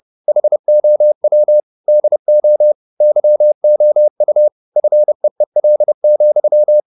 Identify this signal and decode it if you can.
Morse code, heard as a keyed tone, about 30 wpm, 600 Hz. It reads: HOW DO YOU FEEL,